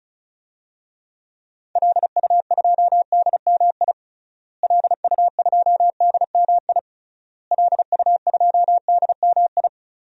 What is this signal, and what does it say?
Morse code, heard as a keyed tone, about 36 wpm, 700 Hz. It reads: LU2DMI LU2DMI LU2DMI
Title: DXCC-100 Call Signs - 6 Characters - Repeated 3x - 35wpm